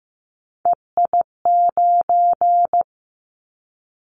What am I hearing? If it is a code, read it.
Morse code, heard as a keyed tone, about 15 wpm, 700 Hz. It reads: EI9
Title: Callsign Prefixes for Top 50 DXCC - 15wpm